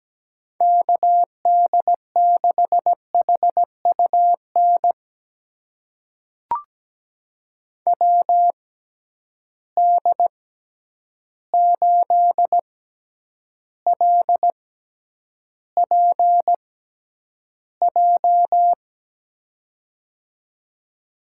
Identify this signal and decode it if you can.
Morse code, heard as a keyed tone, about 17 wpm, spaced as for 9 wpm, 700 Hz. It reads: KD6HUN WD8LPJ
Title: USA Call Signs - 6x Character Spacing - 17wpm